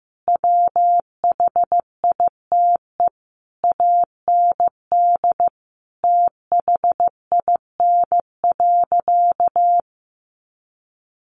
Morse code, heard as a keyed tone, about 15 wpm, 700 Hz. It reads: WHITE AND THIN.
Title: Sentences from Top 1200 Words 15wpm